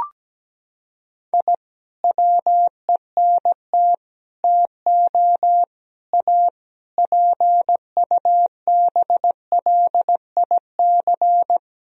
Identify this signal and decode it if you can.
Morse code, heard as a keyed tone, about 17 wpm, 700 Hz. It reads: I WENT TO A PUBLIC
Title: Sentences from Top 1000 Words - Review Part 3 - 17wpm